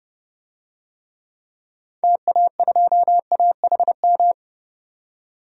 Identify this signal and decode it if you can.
Morse code, heard as a keyed tone, about 30 wpm, 700 Hz. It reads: TA2AHM
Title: DXCC-100 Call Signs - 6 Characters - 30wpm